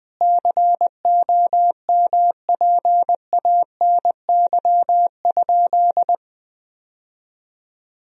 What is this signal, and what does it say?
Morse code, heard as a keyed tone, about 20 wpm, 700 Hz. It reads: COMPANY?
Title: Sentences from Top 1000 Words - Review Part 3 - 20wpm